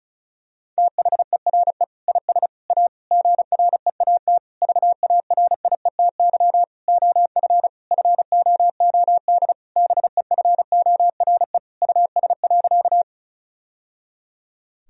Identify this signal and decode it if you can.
Morse code, heard as a keyed tone, about 36 wpm, 700 Hz. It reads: THERE IS A GREAT VARIETY OF FOOD BEFORE US.
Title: Sentences from Top 1000 Words - Review Part 4 - 35wpm